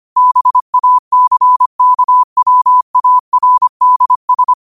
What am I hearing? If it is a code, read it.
Morse code, heard as a keyed tone, about 25 wpm, 1000 Hz. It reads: BACKWARDS
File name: qotdfiles / 25wpm